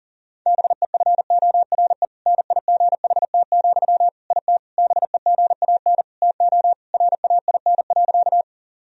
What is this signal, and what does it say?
Morse code, heard as a keyed tone, about 40 wpm, 700 Hz. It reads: BEFORE NIGHT, IT BEGAN TO RAIN.
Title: Sentences from Top 1000 Words - Review Part 1 - 40wpm